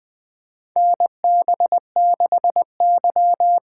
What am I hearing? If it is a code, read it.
Morse code, heard as a keyed tone, about 20 wpm, 700 Hz. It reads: NB6Y
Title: USA Call Signs - 4 Characters - 8x Character Spacing - 20wpm